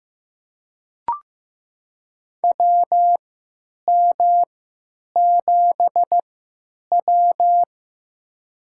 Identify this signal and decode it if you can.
Morse code, heard as a keyed tone, about 15 wpm, spaced as for 9 wpm, 700 Hz. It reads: WM7W
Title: USA Call Signs - 3x Character Spacing - 15wpm